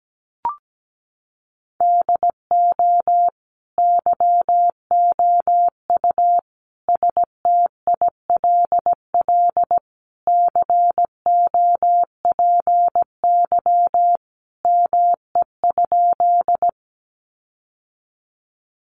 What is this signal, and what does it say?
Morse code, heard as a keyed tone, about 17 wpm, 700 Hz. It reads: DO YOU STILL COPY ME?
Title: Sentences from Top 1400 Words 17wpm